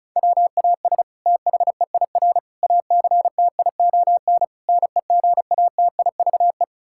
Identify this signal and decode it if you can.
Morse code, heard as a keyed tone, about 36 wpm, 700 Hz. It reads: WAS THEIR ACTION NEGATIVE
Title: Sentences from Top 1000 Words - Review Part 6 - 35wpm